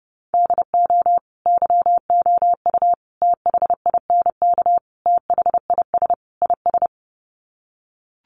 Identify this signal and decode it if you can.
Morse code, heard as a keyed tone, about 30 wpm, 700 Hz. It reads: DO YOU THINK THIS IS
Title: Sets of 5 Words Encore2 30wpm